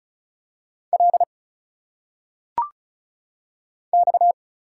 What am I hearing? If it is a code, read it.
Morse code, heard as a keyed tone, about 36 wpm, 700 Hz. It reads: L X
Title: Single Letters - Phonetic Pronunciation 35wpm